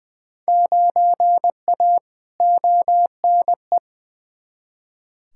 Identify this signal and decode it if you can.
Morse code, heard as a keyed tone, about 20 wpm, 700 Hz. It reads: 9A ONE
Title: Field Day Exchanges Encore - 20wpm